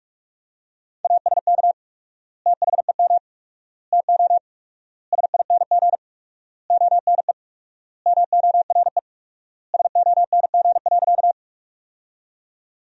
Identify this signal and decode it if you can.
Morse code, heard as a keyed tone, about 41 wpm, 700 Hz. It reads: ASK THEM TO SING ONE MORE SONG.
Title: 4x Word-Spacing - Sentences from Top 400 Words - 45wpm